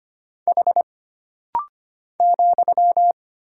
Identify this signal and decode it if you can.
Morse code, heard as a keyed tone, about 25 wpm, 700 Hz. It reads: H ,
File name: Single Character (Letter:Number:Punctuation:Prosign) - Rapid-Fire - 25wpm